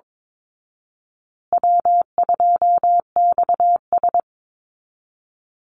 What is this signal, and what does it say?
Morse code, heard as a keyed tone, about 22 wpm, 700 Hz. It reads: W2XS
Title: Easy Call Signs Repeated 3x - 22wpm